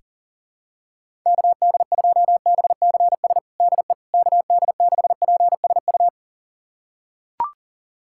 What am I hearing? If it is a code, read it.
Morse code, heard as a keyed tone, about 40 wpm, 700 Hz. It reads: KD2BCS DE KD6PSU
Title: USA Callsign DE Callsign 40wpm